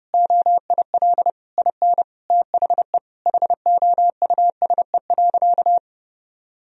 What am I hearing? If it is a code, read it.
Morse code, heard as a keyed tone, about 30 wpm, 700 Hz. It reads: OIL IN THE HOUSE.
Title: Sentences from Top 1000 Words - Review Part 4 - 30wpm